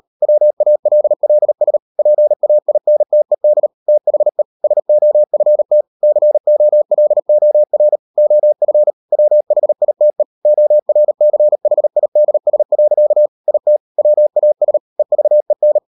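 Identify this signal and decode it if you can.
Morse code, heard as a keyed tone, about 38 wpm, 600 Hz. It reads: WALLS PAINTED THE SOFT COLOR OF WHITE ORCHIDS. IT WAS EVEN
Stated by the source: Morse code practice at 38 WPM, sending Ken Follett's novel Never in English, from CW med Gnister.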